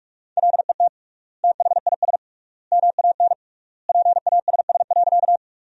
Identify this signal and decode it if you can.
Morse code, heard as a keyed tone, about 46 wpm, 700 Hz. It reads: LET THIS MAN PASS.
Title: 3x Word-Spacing - Sentences from Top 400 Words - 45wpm